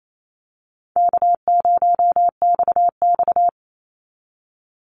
Morse code, heard as a keyed tone, about 28 wpm, 700 Hz. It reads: K0XX
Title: Easy Call Signs - 28wpm